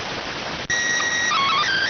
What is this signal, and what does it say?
SSTV (slow-scan television), PD 290.